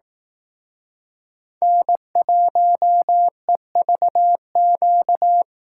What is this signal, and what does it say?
Morse code, heard as a keyed tone, about 18 wpm, 700 Hz. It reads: N1EVQ
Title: Speed Racing - USA 1x3 Call Signs 15wpm